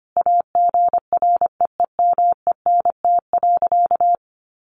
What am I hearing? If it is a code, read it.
Morse code, heard as a keyed tone, about 25 wpm, 700 Hz. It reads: AGREEMENT.
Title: Sentences from Top 1500 Words 25wpm